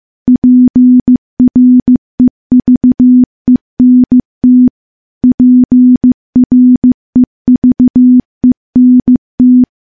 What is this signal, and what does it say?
Morse code, heard as a keyed tone, about 14 wpm, 250 Hz. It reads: PREVENT PREVENT
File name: morse-training-15WPM